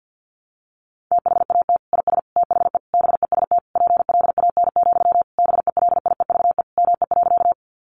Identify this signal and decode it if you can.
Morse code, heard as a keyed tone, about 50 wpm, 700 Hz. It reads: THAT IS THE BEST PLAN, BELIEVE ME.
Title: Sentences from Top 1000 Words 50wpm